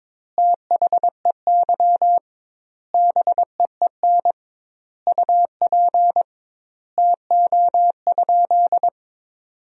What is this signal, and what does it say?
Morse code, heard as a keyed tone, about 22 wpm, 700 Hz. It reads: THEY BEEN UP TO?
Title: 2x Word-Spacing - Sentences from Top 100 Words - 22wpm